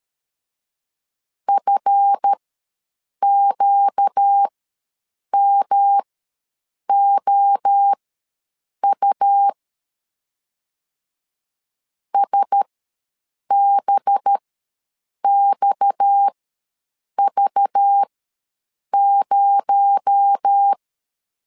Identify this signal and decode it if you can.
Morse code, heard as a keyed tone, about 13 wpm, spaced as for 7 wpm, 800 Hz. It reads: FQMOU SBXV0
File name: LEZIONI / 8 YJ45 / 8 YJ ws 12wpm 3342